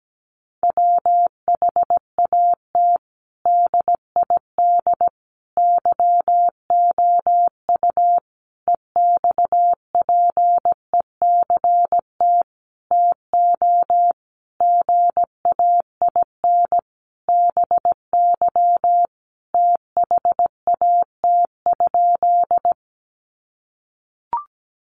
Morse code, heard as a keyed tone, about 17 wpm, 700 Hz. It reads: WHAT DID YOU EXPECT TO GAIN BY THAT?
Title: Sentences from Top 1100 Words 17wpm